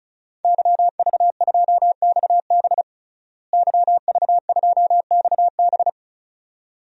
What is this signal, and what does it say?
Morse code, heard as a keyed tone, about 36 wpm, 700 Hz. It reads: YV2XB YV2XB
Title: DXCC-100 Call Signs - 5 Characters - Repeated 2x - 35wpm